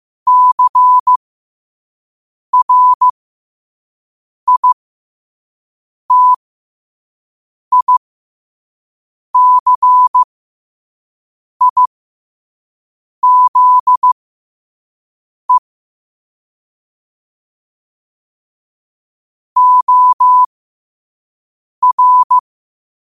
Morse code, heard as a keyed tone, about 15 wpm, spaced as for 5 wpm, 1000 Hz. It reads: CRITICIZE OR
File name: qotdfiles / 5wpm